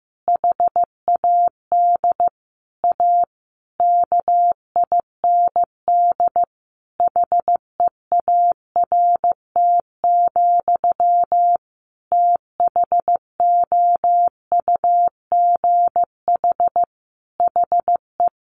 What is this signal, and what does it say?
Morse code, heard as a keyed tone, about 15 wpm, 700 Hz. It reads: HAD A KIND HEART, THOUGH HE
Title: Sentences from Top 1400 Words 15wpm